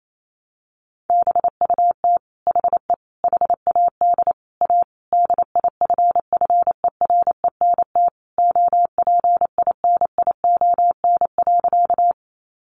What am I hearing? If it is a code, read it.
Morse code, heard as a keyed tone, about 28 wpm, 700 Hz. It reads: BUT HE HAD A DIFFERENT OPINION.
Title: Sentences from Top 1200 Words 28wpm